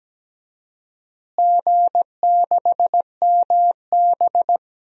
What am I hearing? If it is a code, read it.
Morse code, heard as a keyed tone, about 17 wpm, 700 Hz. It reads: G6MB
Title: DXCC-100 Call Signs - 4 Characters - Repeated 2x - 17wpm